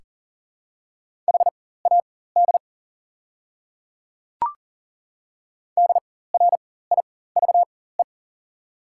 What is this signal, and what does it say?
Morse code, heard as a keyed tone, about 40 wpm, spaced as for 19 wpm, 690 Hz. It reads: HAD DRIVE
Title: 4x Character Spacing - Top 500 Words - 40wpm